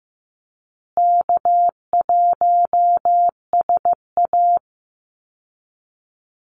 Morse code, heard as a keyed tone, about 14 wpm, 700 Hz. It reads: K1SA
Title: Easy Call Signs Repeated 3x - 15wpm